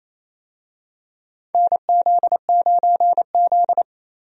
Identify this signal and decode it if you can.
Morse code, heard as a keyed tone, about 28 wpm, 690 Hz. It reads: NZ9Z
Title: USA Call Signs - 4 Characters - 28wpm